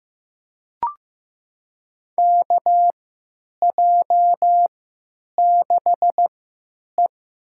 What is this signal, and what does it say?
Morse code, heard as a keyed tone, about 16 wpm, spaced as for 9 wpm, 700 Hz. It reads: KJ6E
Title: USA Call Signs - 6 Characters - 3x Character Spacing - 15wpm